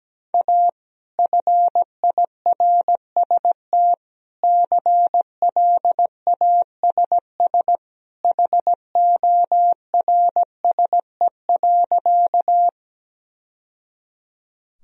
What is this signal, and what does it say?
Morse code, heard as a keyed tone, about 17 wpm, 700 Hz. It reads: A FIRST CLASS HORSE.
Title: Sentences from Top 1000 Words - Review Part 2 - 17wpm